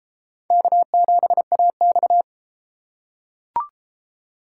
Morse code, heard as a keyed tone, about 30 wpm, 700 Hz. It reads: K7AX